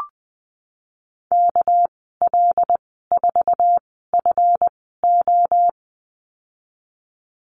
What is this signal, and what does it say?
Morse code, heard as a keyed tone, about 20 wpm, spaced as for 15 wpm, 700 Hz. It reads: KL4FO
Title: 207 Lesson 33g - Letter j - Call Signs 20wpm